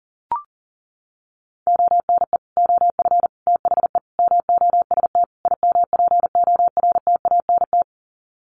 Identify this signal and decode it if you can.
Morse code, heard as a keyed tone, about 40 wpm, 700 Hz. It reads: ONE OF THE MOST IMPORTANT
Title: Sets of 5 Words 40wpm